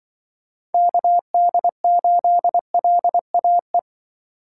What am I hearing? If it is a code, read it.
Morse code, heard as a keyed tone, about 24 wpm, 700 Hz. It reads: KD8LAE
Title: Speed Racing - USA 2x3 Call Signs 20wpm